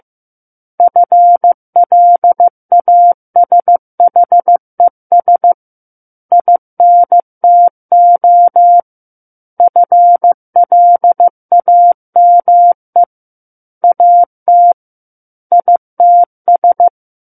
Morse code, heard as a keyed tone, about 15 wpm, 700 Hz. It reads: FLASHES INTO FLAME AT ITS
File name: Chapter 6 - 15 WPM